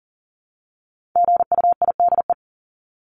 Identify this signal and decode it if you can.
Morse code, heard as a keyed tone, about 40 wpm, 700 Hz. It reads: GUIDE
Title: Top 501-750 Words 40wpm